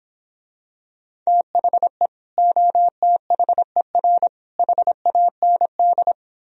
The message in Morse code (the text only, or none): THE OTHER HAND